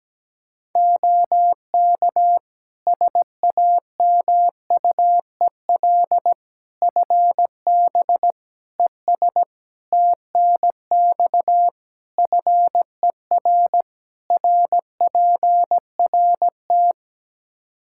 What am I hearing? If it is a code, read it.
Morse code, heard as a keyed tone, about 17 wpm, 700 Hz. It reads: OK SAMUEL FB ES TNX FER RPRT
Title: Practice QSO - Exchange 2 - Info - 17wpm